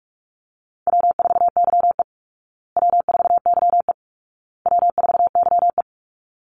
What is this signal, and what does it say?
Morse code, heard as a keyed tone, about 44 wpm, 700 Hz. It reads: W4YE W4YE W4YE
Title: Call Signs Repeated 2x - 45wpm